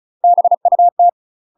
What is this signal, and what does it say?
Morse code, heard as a keyed tone, about 36 wpm, 700 Hz. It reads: BUT
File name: UK MASTER QSOs TRAINING / 35WPM / UK MASTER QSOs 0038